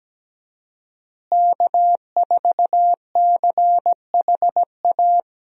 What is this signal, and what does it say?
Morse code, heard as a keyed tone, about 17 wpm, 700 Hz. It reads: K4CHA
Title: USA Call Signs - 5 Characters - Repeated 2x - 17wpm